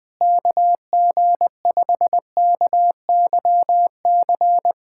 Morse code, heard as a keyed tone, about 20 wpm, 700 Hz. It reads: KG5KYC
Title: USA Callsign DE Callsign - 6 Characters - 20wpm